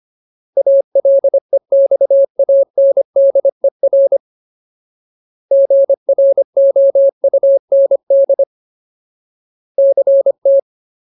Morse code, heard as a keyed tone, about 25 wpm, 550 Hz. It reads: ALEXANDER GROUND CT
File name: Ultimate Word List / 25WPM Ultimate